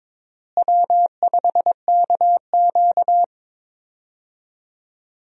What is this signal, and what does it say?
Morse code, heard as a keyed tone, about 21 wpm, 700 Hz. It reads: W5KQ
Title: USA Call Signs - 4 Characters - Repeated 3x - 22wpm